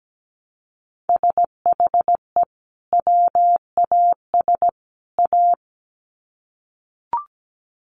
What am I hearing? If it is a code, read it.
Morse code, heard as a keyed tone, about 17 wpm, 700 Hz. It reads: SHE WAS A
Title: Sets of 3 Words 17wpm